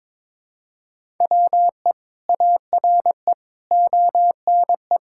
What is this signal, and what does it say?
Morse code, heard as a keyed tone, about 22 wpm, 700 Hz. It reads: WE ARE ONE
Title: Sentences from Top 1600 Words 22wpm